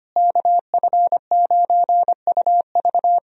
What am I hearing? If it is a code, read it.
Morse code, heard as a keyed tone, about 25 wpm, 700 Hz. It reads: KF9UV
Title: USA Call Signs - 5 Characters - Repeated 3x - 25wpm